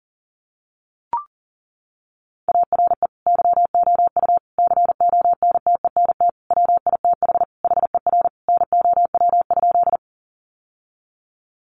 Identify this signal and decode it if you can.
Morse code, heard as a keyed tone, about 40 wpm, 700 Hz. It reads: ARE YOU CONTENT WITH HER NOW?
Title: Sentences from Top 900 Words 40wpm